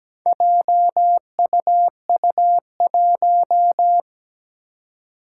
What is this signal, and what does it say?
Morse code, heard as a keyed tone, about 17 wpm, 700 Hz. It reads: JUU1
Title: Four Challenging Letter-Number Combinations 17wpm